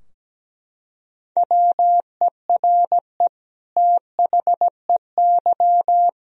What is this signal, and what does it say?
Morse code, heard as a keyed tone, about 17 wpm, 700 Hz. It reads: WERE THEY
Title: Sentences from Top 1000 Words - Review Part 3 - 17wpm